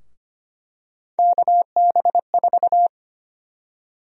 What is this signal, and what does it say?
Morse code, heard as a keyed tone, about 25 wpm, 700 Hz. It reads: KB4
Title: USA Call Sign Prefixes - 25wpm